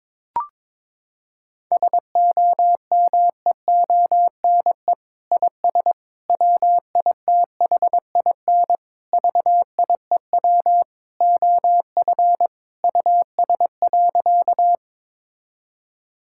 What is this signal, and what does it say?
Morse code, heard as a keyed tone, about 22 wpm, 700 Hz. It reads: SOMEONE IS WITHIN VIEW OF US.
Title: Sentences from Top 1000 Words - Review Part 6 - 22wpm